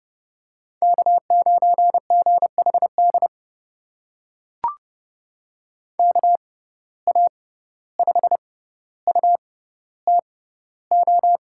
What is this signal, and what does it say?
Morse code, heard as a keyed tone, about 30 wpm, spaced as for 15 wpm, 700 Hz. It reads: K9GHD KA5UTO
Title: USA Call Signs - 6x Character Spacing - 30wpm